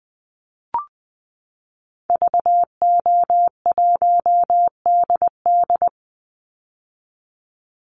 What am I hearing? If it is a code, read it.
Morse code, heard as a keyed tone, about 20 wpm, 700 Hz. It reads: VO1DD
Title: Call Signs - 20wpm